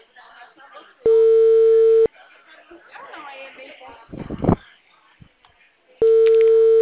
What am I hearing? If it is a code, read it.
Ringback tone (China, 450 Hz), with speech.
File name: out- -2016